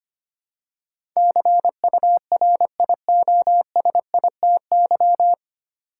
Morse code, heard as a keyed tone, about 25 wpm, 690 Hz. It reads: CURIOSITY
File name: practice / 25wpm / 9 Letter Words - First 3 Letters Given - 25wpm